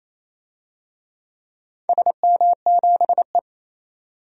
Morse code, heard as a keyed tone, about 28 wpm, 700 Hz. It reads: SM7E
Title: DXCC-100 Call Signs - 4 Characters - 28wpm